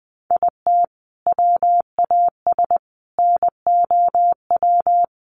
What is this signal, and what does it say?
Morse code, heard as a keyed tone, about 20 wpm, 700 Hz. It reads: IT WAS NOW
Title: Sentences from Top 1300 Words 20wpm